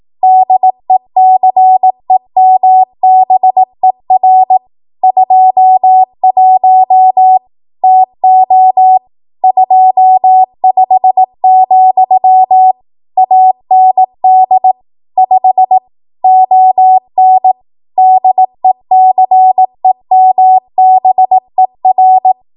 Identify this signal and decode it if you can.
Morse code, heard as a keyed tone, about 18 wpm, 750 Hz. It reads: DECEMBER 21 TO 25, AND 5 ON DECEMBER
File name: Archive / 18 WPM / 18WPM